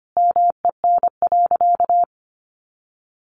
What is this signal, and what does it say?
Morse code, heard as a keyed tone, about 25 wpm, 700 Hz. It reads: MEN.
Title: Sentences from Top 1500 Words 25wpm